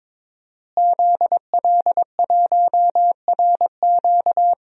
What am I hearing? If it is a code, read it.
Morse code, heard as a keyed tone, about 22 wpm, 700 Hz. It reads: ZL1RQ
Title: DXCC-100 Call Signs - 5 Characters - Repeated 2x - 22wpm